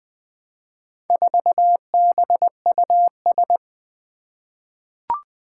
Morse code, heard as a keyed tone, about 20 wpm, 700 Hz. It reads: 4BUS
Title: Four Challenging Letter-Number Combinations 20wpm